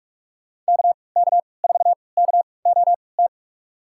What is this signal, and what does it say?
Morse code, heard as a keyed tone, about 45 wpm, spaced as for 25 wpm, 700 Hz. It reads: KK4KOT
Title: USA Call Signs - 6 Characters - 3x Character Spacing - 45wpm